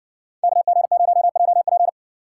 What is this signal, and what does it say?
Morse code, heard as a keyed tone, about 58 wpm, 700 Hz. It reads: KG0JP